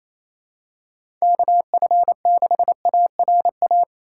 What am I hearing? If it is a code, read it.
Morse code, heard as a keyed tone, about 28 wpm, 700 Hz. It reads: KF6ARA
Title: USA Call Signs - 6 Characters - 4x Character Spacing - 28wpm